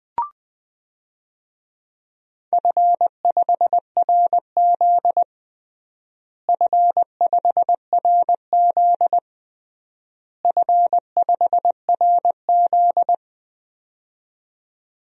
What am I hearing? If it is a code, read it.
Morse code, heard as a keyed tone, about 20 wpm, 700 Hz. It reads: F5RZ F5RZ F5RZ
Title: DXCC-100 Call Signs - 4 Characters - Repeated 3x - 20wpm